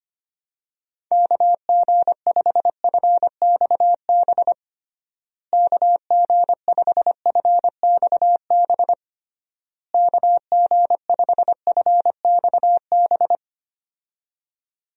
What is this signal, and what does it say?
Morse code, heard as a keyed tone, about 25 wpm, 700 Hz. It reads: KG5FXB KG5FXB KG5FXB
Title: USA Call Signs - 6 Characters - Repeated 3x - 25wpm